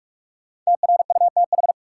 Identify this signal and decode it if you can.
Morse code, heard as a keyed tone, about 46 wpm, 700 Hz. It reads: TRUTH
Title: Sets of 5 Words - New Word Drill - 45wpm